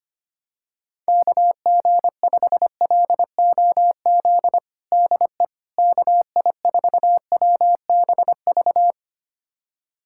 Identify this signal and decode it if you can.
Morse code, heard as a keyed tone, about 25 wpm, 700 Hz. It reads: KG5LOZ DE KI4WBV
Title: USA Callsign DE Callsign - 6 Characters - 25wpm